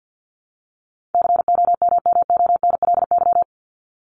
Morse code, heard as a keyed tone, about 50 wpm, 700 Hz. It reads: COMMONLY
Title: LY Suffix - Top 100 Words with Base Words 50wpm